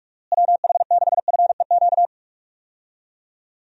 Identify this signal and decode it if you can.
Morse code, heard as a keyed tone, about 46 wpm, 700 Hz. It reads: WH6FEQ
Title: USA Call Signs - 6 Characters - Repeated 2x - 45wpm